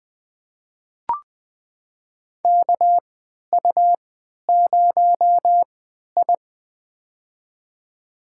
Morse code, heard as a keyed tone, about 21 wpm, spaced as for 12 wpm, 700 Hz. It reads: KU0I
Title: USA Call Signs - 3x Character Spacing - 20wpm